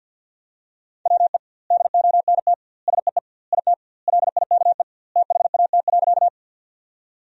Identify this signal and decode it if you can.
Morse code, heard as a keyed tone, about 53 wpm, 700 Hz. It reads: WE DONT SEE IT LIKE THAT.
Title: 2x Word-Spacing - Sentences from Top 300 Words - 50wpm